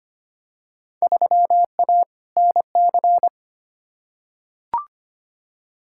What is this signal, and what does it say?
Morse code, heard as a keyed tone, about 25 wpm, 700 Hz. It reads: 3A NC
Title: Field Day Exchanges Encore - 25wpm